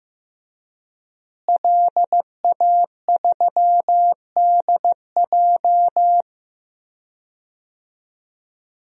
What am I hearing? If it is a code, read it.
Morse code, heard as a keyed tone, about 15 wpm, 700 Hz. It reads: LA3DJ
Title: DXCC-100 Call Signs - 5 Characters - Repeated 2x - 15wpm